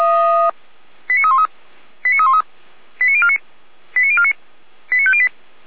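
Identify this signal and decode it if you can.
Five-tone selective call (ZVEI), heard as quick runs of five tones, tones of about 70 ms. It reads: 89313 89313 89049 89049 89508
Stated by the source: ZVEI five-tone call 89270 received on SDR.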